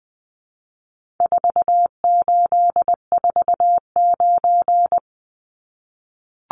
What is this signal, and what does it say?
Morse code, heard as a keyed tone, about 20 wpm, 700 Hz. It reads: 4849
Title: Numbers - 4-Digit 20wpm